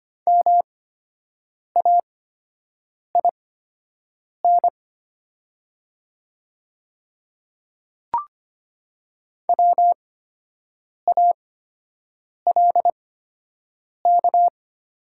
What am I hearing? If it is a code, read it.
Morse code, heard as a keyed tone, about 25 wpm, spaced as for 7 wpm, 690 Hz. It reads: MAIN WALK
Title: 8x Character Spacing - Top 500 Words - 25wpm